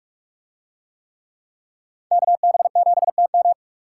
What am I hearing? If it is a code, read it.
Morse code, heard as a keyed tone, about 42 wpm, 700 Hz. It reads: KB7TM